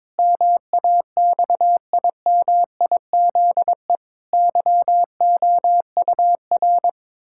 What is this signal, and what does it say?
Morse code, heard as a keyed tone, about 22 wpm, 700 Hz. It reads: MAXIMIZE YOUR
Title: Pangram Challenge v2 - 22wpm